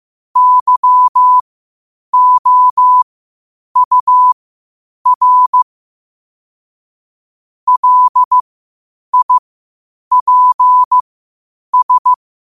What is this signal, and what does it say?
Morse code, heard as a keyed tone, about 15 wpm, spaced as for 8 wpm, 1000 Hz. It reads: YOUR LIPS
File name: qotdfiles / 8wpm